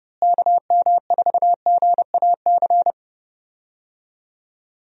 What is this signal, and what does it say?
Morse code, heard as a keyed tone, about 30 wpm, 700 Hz. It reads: KM4GAC
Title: USA Call Signs - 6 Characters - Repeated 3x - 30wpm